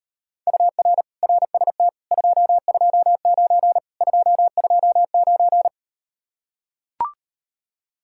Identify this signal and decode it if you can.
Morse code, heard as a keyed tone, about 38 wpm, 700 Hz. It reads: UR RST 229 229